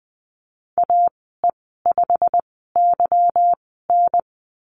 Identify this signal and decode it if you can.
Morse code, heard as a keyed tone, about 20 wpm, spaced as for 15 wpm, 700 Hz. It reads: AE5YN
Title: 173 Lesson 28g - Number 9 - Call Signs 20wpm